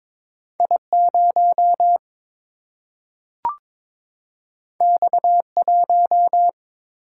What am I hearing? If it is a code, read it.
Morse code, heard as a keyed tone, about 21 wpm, 700 Hz. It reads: I 0 X 1
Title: Single-Letter Single-Number 22wpm